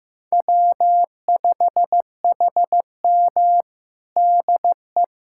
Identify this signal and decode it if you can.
Morse code, heard as a keyed tone, about 15 wpm, 700 Hz. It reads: W5HM DE
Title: USA Callsign DE Callsign - 4 Characters - 15wpm